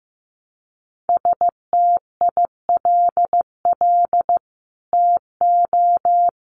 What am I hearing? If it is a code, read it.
Morse code, heard as a keyed tone, about 15 wpm, 700 Hz. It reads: STILL TO
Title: Sets of 5 Words Encore2 15wpm